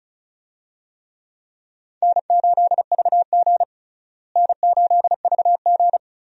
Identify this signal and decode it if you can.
Morse code, heard as a keyed tone, about 36 wpm, 700 Hz. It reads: N8VG N8VG
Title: USA Call Signs - 4 Characters - Repeated 2x - 35wpm